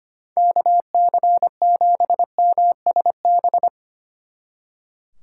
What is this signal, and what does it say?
Morse code, heard as a keyed tone, about 30 wpm, 700 Hz. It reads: KC7MSB
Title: Speed Racing - USA 2x3 Call Signs 25wpm